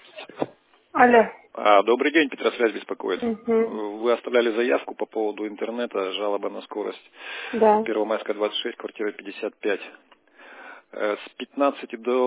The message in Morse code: none

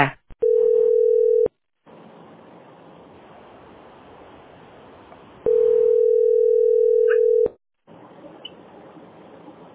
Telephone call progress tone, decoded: ringback tone

